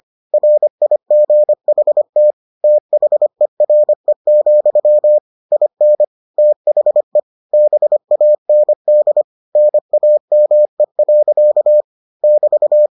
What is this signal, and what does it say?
Morse code, heard as a keyed tone, about 25 wpm, 600 Hz. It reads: RIGHT THERE, IN THE BAND NAME. <BT>